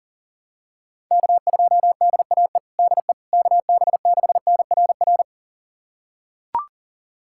Morse code, heard as a keyed tone, about 40 wpm, 700 Hz. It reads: K2DAE DE KB6NRR
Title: USA Callsign DE Callsign 40wpm